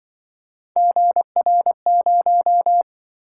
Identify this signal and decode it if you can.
Morse code, heard as a keyed tone, about 25 wpm, 700 Hz. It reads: GR0